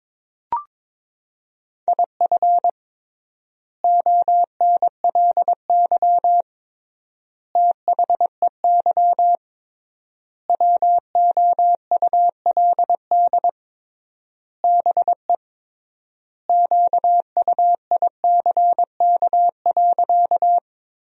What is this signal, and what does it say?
Morse code, heard as a keyed tone, about 22 wpm, 700 Hz. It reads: IF ONLY THEY WOULD BE QUICK.
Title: 3x Word-Spacing - Sentences from Top 500 Words - 22wpm